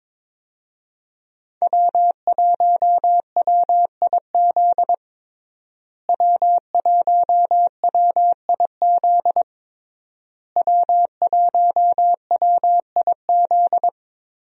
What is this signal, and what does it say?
Morse code, heard as a keyed tone, about 22 wpm, 700 Hz. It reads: W1WIZ W1WIZ W1WIZ
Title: USA Call Signs - 5 Characters - Repeated 3x - 22wpm